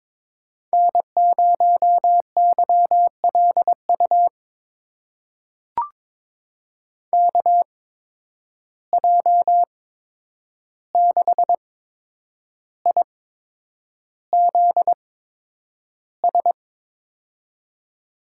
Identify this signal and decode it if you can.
Morse code, heard as a keyed tone, about 22 wpm, spaced as for 9 wpm, 700 Hz. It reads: N0YLU KJ6IZS
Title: USA Call Signs - 8x Character Spacing - 22wpm